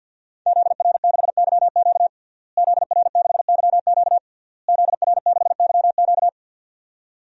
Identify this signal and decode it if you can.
Morse code, heard as a keyed tone, about 48 wpm, 700 Hz. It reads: ZR6YQ ZR6YQ ZR6YQ
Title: DXCC-100 Call Signs - 5 Characters - Repeated 3x - 50wpm